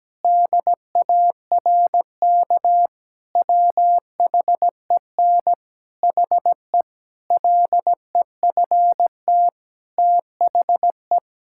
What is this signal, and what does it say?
Morse code, heard as a keyed tone, about 17 wpm, 700 Hz. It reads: DARK WHEN HE LEFT THE
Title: Sentences from Top 1000 Words - Review Part 5 - 17wpm